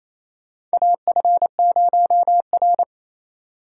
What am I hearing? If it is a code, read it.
Morse code, heard as a keyed tone, about 27 wpm, 700 Hz. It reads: AF0R
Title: USA Call Signs - 4 Characters - 4x Character Spacing - 28wpm